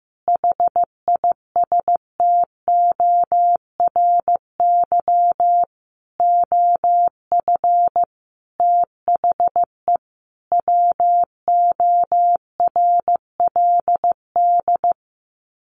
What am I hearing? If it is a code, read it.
Morse code, heard as a keyed tone, about 14 wpm, 700 Hz. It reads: HISTORY OF THE WORLD
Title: Sets of 5 Words Encore2 15wpm